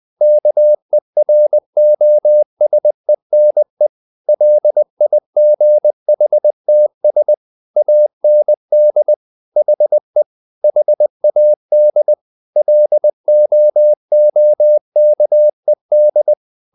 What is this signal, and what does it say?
Morse code, heard as a keyed tone, about 20 wpm, 600 Hz. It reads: KEROSENE LIGHTS AND HE HAD LOOKED